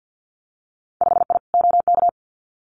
Morse code, heard as a keyed tone, about 50 wpm, 700 Hz. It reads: 5I OK
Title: Winter Field Day Exchanges Encore 50wpm